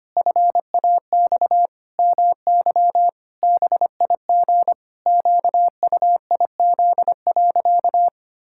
Morse code, heard as a keyed tone, about 25 wpm, 700 Hz. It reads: FAX MY BIG QUIZ.
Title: Pangram Challenge v2 - 25wpm